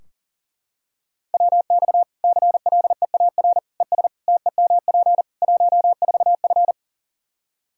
Morse code, heard as a keyed tone, about 40 wpm, 700 Hz. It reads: WX CLEAR ES TEMP 14F
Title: K1USN SST QSO Practice 40wpm